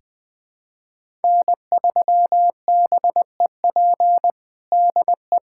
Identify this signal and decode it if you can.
Morse code, heard as a keyed tone, about 20 wpm, 700 Hz. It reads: N3BEP DE
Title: USA Callsign DE Callsign - 5 Characters - 20wpm